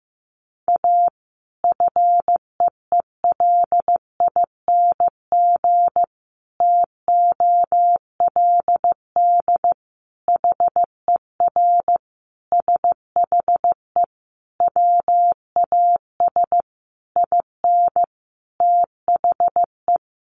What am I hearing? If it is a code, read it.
Morse code, heard as a keyed tone, about 15 wpm, 700 Hz. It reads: A FEELING TOLD HER SHE WAS IN THE
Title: Sentences from Top 1200 Words 15wpm